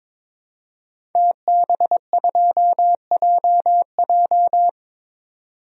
Morse code, heard as a keyed tone, about 22 wpm, 700 Hz. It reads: TB2JJ
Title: DXCC-100 Call Signs - 5 Characters - 22wpm